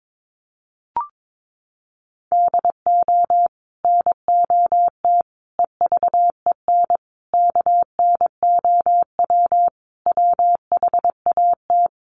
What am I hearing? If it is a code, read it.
Morse code, heard as a keyed tone, about 22 wpm, 700 Hz. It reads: DO NOT EVEN KNOW WHAT
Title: Sets of 5 Words Encore 22wpm